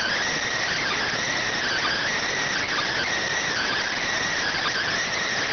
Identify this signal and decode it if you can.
Emergency Alert System (SAME) tones (header bursts), heard as harsh buzzing bursts. It is heard from 0.3 to 4.8 s.